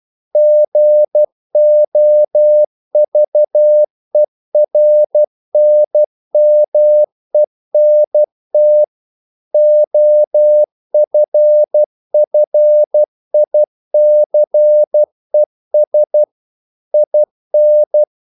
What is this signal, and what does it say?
Morse code, heard as a keyed tone, about 12 wpm, 600 Hz. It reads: GOVERNMENT OFFICES IN
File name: CW / Practice / NationalCapitals12